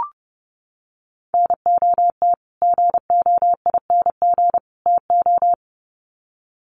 Morse code, heard as a keyed tone, about 30 wpm, 700 Hz. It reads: NOT GOING TO
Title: Sets of 3 Words 30wpm